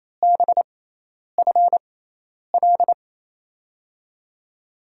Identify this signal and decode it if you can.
Morse code, heard as a keyed tone, about 28 wpm, spaced as for 10 wpm, 700 Hz. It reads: BFL
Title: USA Call Signs - 6 Characters - 6x Character Spacing - 28wpm